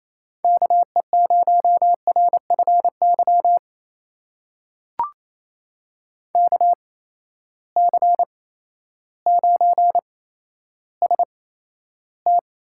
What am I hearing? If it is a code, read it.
Morse code, heard as a keyed tone, about 28 wpm, spaced as for 13 wpm, 700 Hz. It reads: KE0RFY KC9ST
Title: USA Call Signs - 6 Characters - 8x Character Spacing - 28wpm